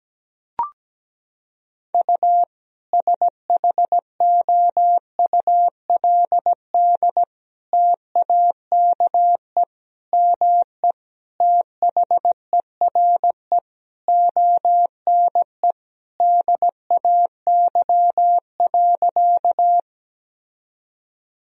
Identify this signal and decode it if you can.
Morse code, heard as a keyed tone, about 17 wpm, 700 Hz. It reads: U SHOULD TAKE ME THERE ONE DAY.
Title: Sentences from Top 300 Words and Abbreviations 17wpm